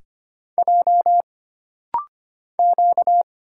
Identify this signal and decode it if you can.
Morse code, heard as a keyed tone, about 25 wpm, 700 Hz. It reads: J Q
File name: Single Character (Letter:Number:Punctuation:Prosign) - Rapid-Fire - 25wpm